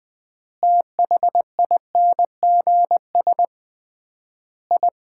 Morse code, heard as a keyed tone, about 21 wpm, 700 Hz. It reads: THINGS I
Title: 3x Word-Spacing - Sets of 5 Words - 20wpm